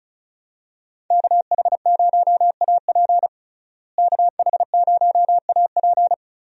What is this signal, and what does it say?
Morse code, heard as a keyed tone, about 34 wpm, 700 Hz. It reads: KH0AP KH0AP
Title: USA Call Signs - 5 Characters - Repeated 3x - 35wpm